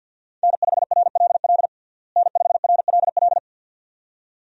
Morse code, heard as a keyed tone, about 51 wpm, 700 Hz. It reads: N5RLL N5RLL
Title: USA Call Signs - 5 Characters - Repeated 2x - 50wpm